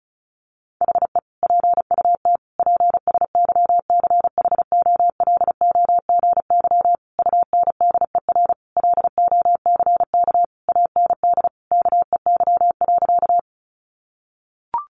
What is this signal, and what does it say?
Morse code, heard as a keyed tone, about 34 wpm, 700 Hz. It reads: HE PUT PSYCHOLOGY UNDER LOCK AND KEY.
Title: Sentences from Top 1400 Words 35wpm